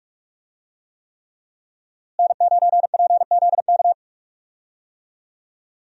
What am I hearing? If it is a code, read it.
Morse code, heard as a keyed tone, about 42 wpm, 700 Hz. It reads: N9PZK
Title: USA Call Signs - 5 Characters - Repeated 3x - 45wpm